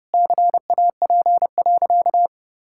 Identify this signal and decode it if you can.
Morse code, heard as a keyed tone, about 30 wpm, 700 Hz. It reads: CAP.
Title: Sentences from Top 1600 Words 30wpm